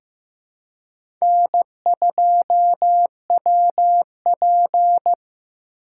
Morse code, heard as a keyed tone, about 15 wpm, 700 Hz. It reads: N2WP
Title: USA Call Signs - 3x Character Spacing - 15wpm